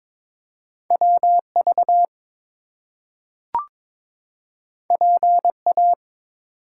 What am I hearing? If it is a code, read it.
Morse code, heard as a keyed tone, about 22 wpm, 700 Hz. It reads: WV PA
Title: US States and Canadian Provinces 22wpm